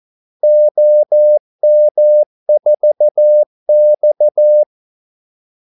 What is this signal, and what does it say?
Morse code, heard as a keyed tone, about 14 wpm, 600 Hz. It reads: OM4X
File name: Callsigns-14wpm